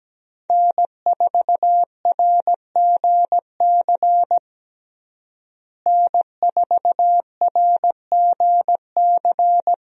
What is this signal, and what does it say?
Morse code, heard as a keyed tone, about 17 wpm, 700 Hz. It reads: N4RGC N4RGC
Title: USA Call Signs - 5 Characters - Repeated 3x - 17wpm